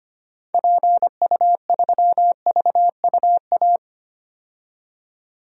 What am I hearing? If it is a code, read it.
Morse code, heard as a keyed tone, about 25 wpm, 690 Hz. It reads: PU3VUA
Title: DXCC-100 Call Signs - 6 Characters - 25wpm